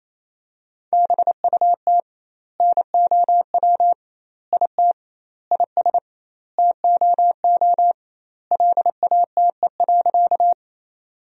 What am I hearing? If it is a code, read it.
Morse code, heard as a keyed tone, about 28 wpm, 700 Hz. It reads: BUT NOW IT IS TOO LATE.
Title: 2x Word-Spacing - Sentences from Top 300 Words - 28wpm